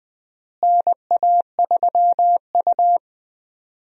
Morse code, heard as a keyed tone, about 21 wpm, 700 Hz. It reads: NA3U
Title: USA Call Signs - 4 Characters - Repeated 3x - 20wpm